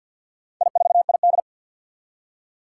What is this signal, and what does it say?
Morse code, heard as a keyed tone, about 52 wpm, 700 Hz. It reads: I4ID
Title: Four Challenging Letter-Number Combinations 50wpm